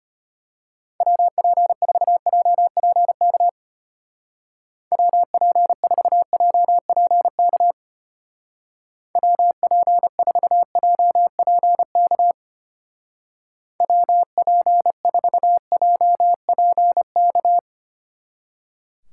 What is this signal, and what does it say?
Morse code, heard as a keyed tone, about 31 wpm, 700 Hz. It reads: WP4JPK WP4JPK WP4JPK WP4JPK
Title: Speed Racing - USA 2x3 Call Signs 25wpm